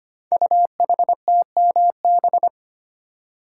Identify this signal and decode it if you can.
Morse code, heard as a keyed tone, about 25 wpm, 700 Hz. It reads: UHTMB
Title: Code Groups - 5 Letter:Numbers - 25wpm